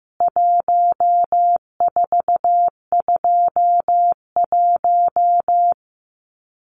Morse code, heard as a keyed tone, about 15 wpm, 700 Hz. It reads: 1421
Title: CWT Exchange 15wpm